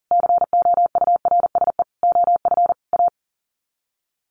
Morse code, heard as a keyed tone, about 40 wpm, 700 Hz. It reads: COURSE OF A
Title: Sets of 5 Words Encore2 40wpm